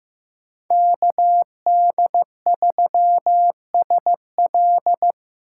Morse code, heard as a keyed tone, about 15 wpm, 700 Hz. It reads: KD3SL
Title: USA Call Signs - 5 Characters - 3x Character Spacing - 15wpm